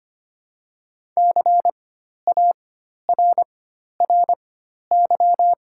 Morse code, heard as a keyed tone, about 25 wpm, spaced as for 12 wpm, 700 Hz. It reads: CARRY
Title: 4x Character Spacing - Top 300 Words - 25wpm